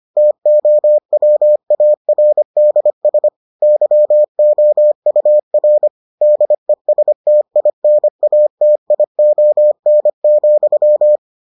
Morse code, heard as a keyed tone, about 25 wpm, 600 Hz. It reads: TOWARDS YOUR DESTINATION,